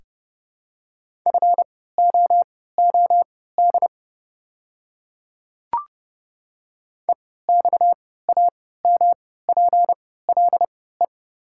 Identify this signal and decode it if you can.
Morse code, heard as a keyed tone, about 30 wpm, spaced as for 17 wpm, 690 Hz. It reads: FOOD EXAMPLE
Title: 3x Character Spacing - Top 300 Words - 30wpm